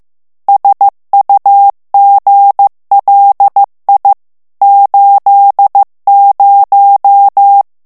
Morse code, heard as a keyed tone, about 15 wpm, 800 Hz. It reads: SUGLI 80
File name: ITALIEN texte clair 1 vitesse 14 mots minute